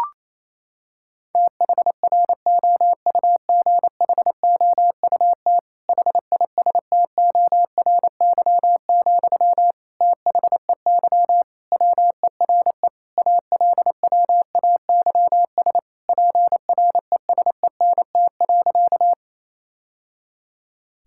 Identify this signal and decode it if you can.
Morse code, heard as a keyed tone, about 27 wpm, 700 Hz. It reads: THROUGHOUT HISTORY, THEY WERE ALWAYS PRESENT.
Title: Sentences from Top 1000 Words - Review Part 5 - 28wpm